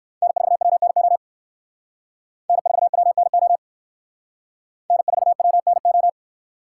Morse code, heard as a keyed tone, about 60 wpm, 700 Hz. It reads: N4WNO N4WNO N4WNO